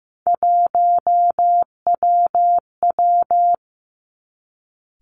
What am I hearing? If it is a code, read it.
Morse code, heard as a keyed tone, about 15 wpm, 700 Hz. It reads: 1WW
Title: Three Challenging Letter-Number Combinations 15wpm